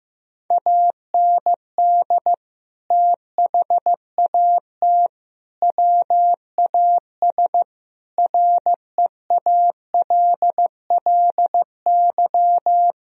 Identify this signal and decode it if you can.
Morse code, heard as a keyed tone, about 15 wpm, 700 Hz. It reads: AND THAT WAS REALLY
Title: Sentences from Top 1000 Words - Review Part 3 - 15wpm